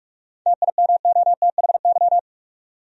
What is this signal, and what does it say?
Morse code, heard as a keyed tone, about 44 wpm, 700 Hz. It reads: TIMOTHY
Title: USA Common Names - 45wpm